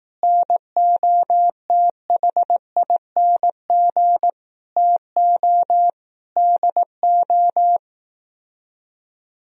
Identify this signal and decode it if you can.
Morse code, heard as a keyed tone, about 18 wpm, 700 Hz. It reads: NOTHING TO DO
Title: Speed-Racing - Sets of 4 Words 15wpm